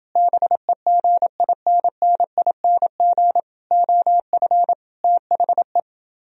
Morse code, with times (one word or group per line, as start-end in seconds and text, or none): 0.2-3.4 BEGINNING
3.7-4.7 OF
5.0-5.8 THE